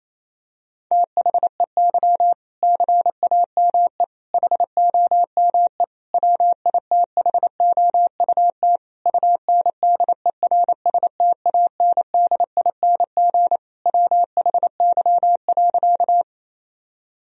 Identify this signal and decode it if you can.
Morse code, heard as a keyed tone, about 28 wpm, 700 Hz. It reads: THEY CAME HOME WITHOUT UNDERSTANDING WHY.
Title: Sentences from Top 1000 Words - Review Part 3 - 28wpm